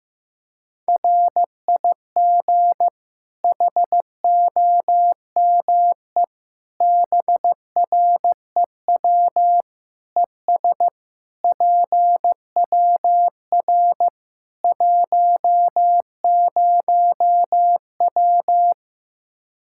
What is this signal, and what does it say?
Morse code, heard as a keyed tone, about 15 wpm, 700 Hz. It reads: RIG HOME BREW ES PWR 10W
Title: Practice QSO - Exchange 2 - Info - 15wpm